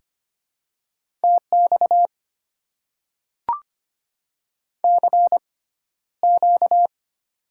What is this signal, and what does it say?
Morse code, heard as a keyed tone, about 25 wpm, spaced as for 13 wpm, 700 Hz. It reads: TX CQ
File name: Top 90 QSO Elements - 6x Character Spacing - 25wpm